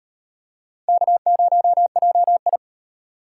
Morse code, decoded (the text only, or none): K0JI